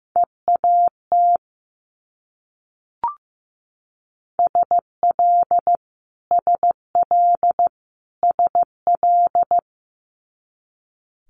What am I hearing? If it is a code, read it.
Morse code, heard as a keyed tone, about 15 wpm, 700 Hz. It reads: EAT SL SL SL
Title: Multi-letter Phonemes Repeated 3x 15wpm